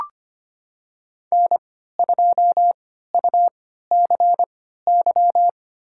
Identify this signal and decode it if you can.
Morse code, heard as a keyed tone, about 25 wpm, spaced as for 14 wpm, 700 Hz. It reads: N2UCY
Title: USA Call Signs - 5 Characters - 3x Character Spacing - 25wpm